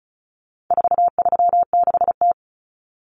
Morse code, heard as a keyed tone, about 34 wpm, 700 Hz. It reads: 436T
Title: Common Cut Numbers - 4-Digit 35wpm